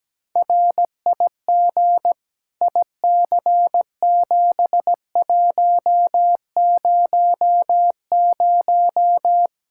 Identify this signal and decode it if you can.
Morse code, heard as a keyed tone, about 17 wpm, 700 Hz. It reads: RIG IC7100
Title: K1USN SST QSO Practice 17wpm